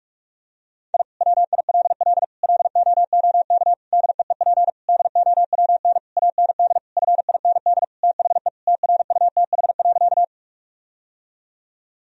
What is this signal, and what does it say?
Morse code, heard as a keyed tone, about 46 wpm, 700 Hz. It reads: I WILL LOOK DEEP DOWN AND FIND THE TRUTH.
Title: Sentences from Top 1000 Words - Review Part 3 - 45wpm